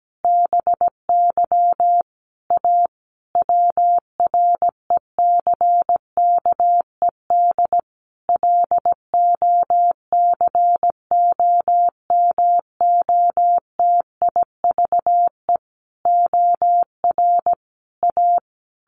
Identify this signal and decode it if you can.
Morse code, heard as a keyed tone, about 17 wpm, 700 Hz. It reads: BY A WRECKED LOCOMOTIVE OR A
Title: A Sound Of Thunder 17wpm